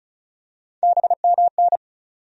Morse code, heard as a keyed tone, about 34 wpm, 700 Hz. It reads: BMN
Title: USA Call Sign Postfixes - 3 Letters 35wpm